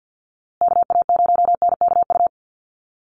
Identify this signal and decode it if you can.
Morse code, heard as a keyed tone, about 50 wpm, 700 Hz. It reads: KA0NKU
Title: USA Call Signs Repeated 3x 50wpm